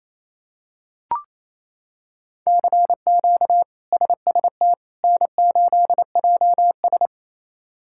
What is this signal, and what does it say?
Morse code, heard as a keyed tone, about 28 wpm, 700 Hz. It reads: CQ SST N8JS
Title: K1USN SST QSO Practice - 28wpm